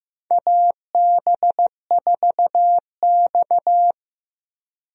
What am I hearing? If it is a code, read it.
Morse code, heard as a keyed tone, about 15 wpm, 700 Hz. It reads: AB4X
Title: USA Call Signs - 4 Characters - Repeated 3x - 15wpm